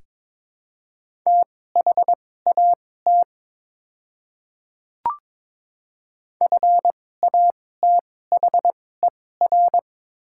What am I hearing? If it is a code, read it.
Morse code, heard as a keyed tone, about 22 wpm, spaced as for 16 wpm, 700 Hz. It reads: THAT FATHER
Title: 2x Character Spacing - Top 300 Words - 22wpm